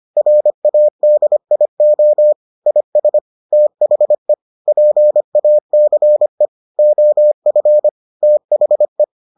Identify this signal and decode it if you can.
Morse code, heard as a keyed tone, about 25 wpm, 600 Hz. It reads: RADIO IS THE PACE OF THE